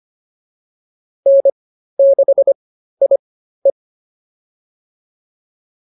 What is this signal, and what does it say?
Morse code, heard as a keyed tone, about 25 wpm, spaced as for 13 wpm, 550 Hz. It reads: N6IE